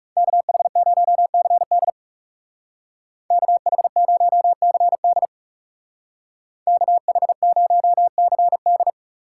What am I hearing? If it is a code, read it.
Morse code, heard as a keyed tone, about 39 wpm, 700 Hz. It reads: KH0CD KH0CD KH0CD